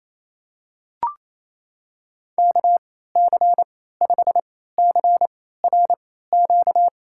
Morse code, heard as a keyed tone, about 28 wpm, spaced as for 16 wpm, 700 Hz. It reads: KC5CRQ
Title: USA Call Signs - 6 Characters - 3x Character Spacing - 28wpm